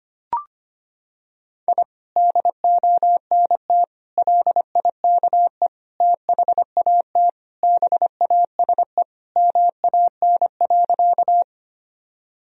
Morse code, heard as a keyed tone, about 25 wpm, 700 Hz. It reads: I DONT LIKE THAT BASE MAN.
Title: Sentences from Top 1000 Words - Review Part 1 - 25wpm